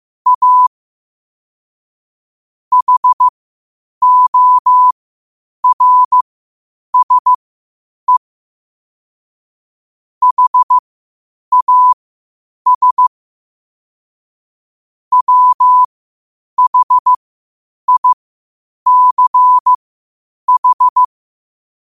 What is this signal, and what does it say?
Morse code, heard as a keyed tone, about 15 wpm, spaced as for 8 wpm, 1000 Hz. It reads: A HORSE HAS WHICH
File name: qotdfiles / 8wpm